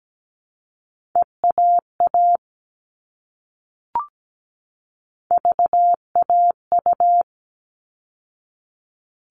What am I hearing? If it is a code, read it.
Morse code, heard as a keyed tone, about 17 wpm, 700 Hz. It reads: EAA VAU
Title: All Cut Numbers - 3-Digit 17wpm